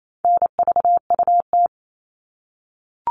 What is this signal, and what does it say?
Morse code, heard as a keyed tone, about 28 wpm, 700 Hz. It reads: NVUT
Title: All Cut Numbers - 4-Digit 28wpm